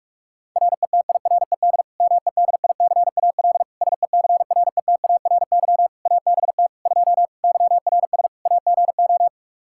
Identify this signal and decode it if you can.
Morse code, heard as a keyed tone, about 46 wpm, 700 Hz. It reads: RETIRED MEDICAL SECRETARY ABT 2 YRS AGO
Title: Practice QSO - Exchange 3 - More Info - 45wpm